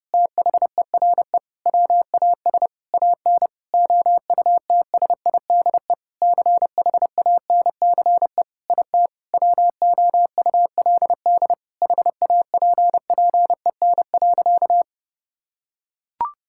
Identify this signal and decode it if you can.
Morse code, heard as a keyed tone, about 30 wpm, 700 Hz. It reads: THERE WAS AN OUTSIDE CHANCE IT WOULD HAPPEN.
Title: Sentences from Top 1000 Words - Review Part 6 - 30wpm